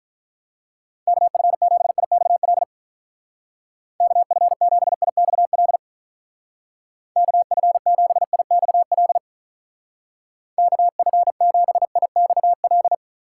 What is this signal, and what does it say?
Morse code, heard as a keyed tone, about 45 wpm, 700 Hz. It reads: KF7IXL KF7IXL KF7IXL KF7IXL